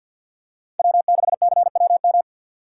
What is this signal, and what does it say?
Morse code, heard as a keyed tone, about 51 wpm, 700 Hz. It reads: W6CWM
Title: K1USN SST QSO Practice - 50wpm